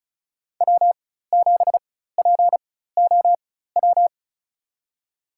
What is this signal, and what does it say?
Morse code, heard as a keyed tone, about 36 wpm, spaced as for 16 wpm, 700 Hz. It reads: W7POW
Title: USA Call Signs - 5 Characters - 4x Character Spacing - 35wpm